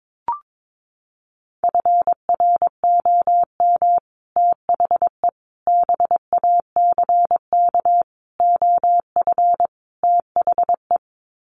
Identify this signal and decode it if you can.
Morse code, heard as a keyed tone, about 22 wpm, 700 Hz. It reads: FROM THE BACK OF THE
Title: Sets of 5 Words Encore 22wpm